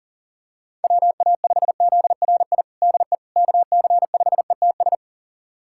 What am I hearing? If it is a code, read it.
Morse code, heard as a keyed tone, about 40 wpm, 700 Hz. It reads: WA5ZRI DE KC5ETS
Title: USA Callsign DE Callsign 40wpm